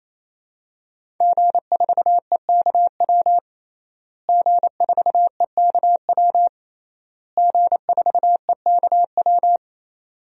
Morse code, heard as a keyed tone, about 28 wpm, 700 Hz. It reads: G4EKW G4EKW G4EKW
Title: DXCC-100 Call Signs - 5 Characters - Repeated 3x - 28wpm